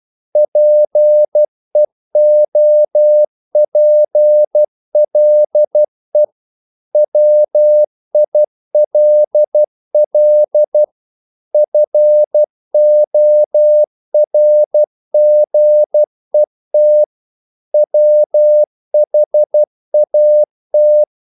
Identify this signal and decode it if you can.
Morse code, heard as a keyed tone, about 12 wpm, 600 Hz. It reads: PEOPLE WILL FORGET WHAT
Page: English quotations in CW (telegraphy - morse)